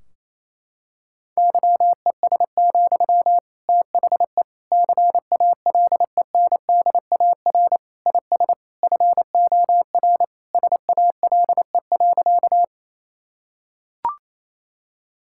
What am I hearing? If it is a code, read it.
Morse code, heard as a keyed tone, about 28 wpm, 700 Hz. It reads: YES, THE CALENDAR IS FOR SALE.
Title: Sentences from Top 1600 Words 28wpm